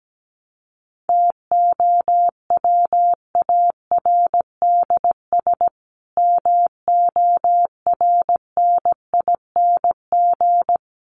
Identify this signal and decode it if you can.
Morse code, heard as a keyed tone, about 17 wpm, 700 Hz. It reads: TOWARDS MORNING
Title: Sentences from Top 1100 Words 17wpm